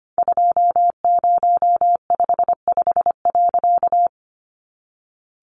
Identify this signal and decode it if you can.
Morse code, heard as a keyed tone, about 25 wpm, 700 Hz. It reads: 2055.
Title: A Sound Of Thunder 25wpm